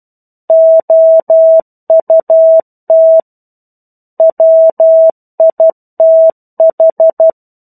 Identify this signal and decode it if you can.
Morse code, heard as a keyed tone, about 12 wpm, 650 Hz. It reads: OUT WITH
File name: 01-Chapter-01 12 WPM